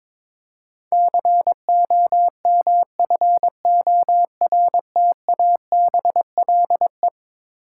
Morse code, heard as a keyed tone, about 22 wpm, 700 Hz. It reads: COMFORTABLE
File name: practice / 22wpm / Eleven Letter Words - 22wpm